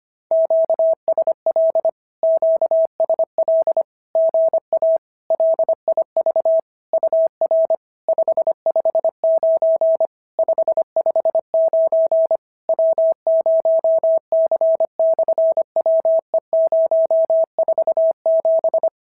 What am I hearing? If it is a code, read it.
Morse code, heard as a keyed tone, about 25 wpm, 650 Hz. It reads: QSL QSL GA LIV UR 559 559 W0C/WE047
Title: SOTA QSO Practice 25wpm